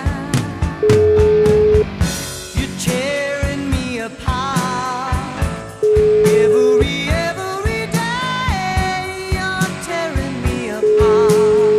Ringback tone (Europe, 425 Hz), with noise.